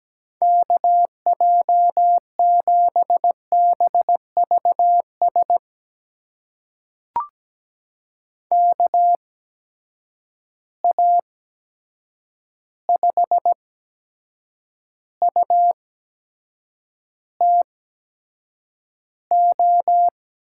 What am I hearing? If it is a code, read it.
Morse code, heard as a keyed tone, about 17 wpm, spaced as for 7 wpm, 700 Hz. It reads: KJ7BVS KA5UTO
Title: USA Call Signs - 6 Characters - 8x Character Spacing - 17wpm